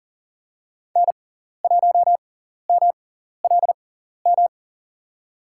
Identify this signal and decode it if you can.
Morse code, heard as a keyed tone, about 40 wpm, spaced as for 14 wpm, 700 Hz. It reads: N1MLM
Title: USA Call Signs - 5 Characters - 6x Character Spacing - 40wpm